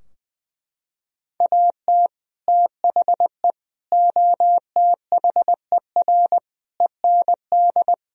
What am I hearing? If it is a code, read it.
Morse code, heard as a keyed tone, about 20 wpm, 700 Hz. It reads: AT THE OTHER END
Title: Sentences from Top 1000 Words - Review Part 2 - 20wpm